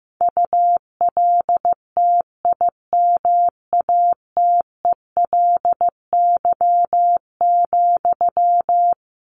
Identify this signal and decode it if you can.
Morse code, heard as a keyed tone, about 15 wpm, 700 Hz. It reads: ULTIMATELY,
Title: Sentences from Top 1200 Words 15wpm